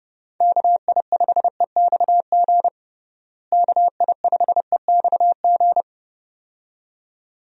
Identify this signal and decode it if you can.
Morse code, heard as a keyed tone, about 30 wpm, 700 Hz. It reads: KI5EXG KI5EXG
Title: USA Call Signs - 6 Characters - Repeated 2x - 30wpm